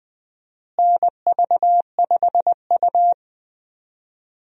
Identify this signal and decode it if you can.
Morse code, heard as a keyed tone, about 21 wpm, 700 Hz. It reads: NV5U
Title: USA Call Signs - 4 Characters - Repeated 3x - 20wpm